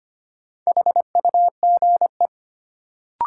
Morse code, heard as a keyed tone, about 25 wpm, 700 Hz. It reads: HUGE
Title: Sentences from Top 1100 Words - New Word Drill Alt - 25wpm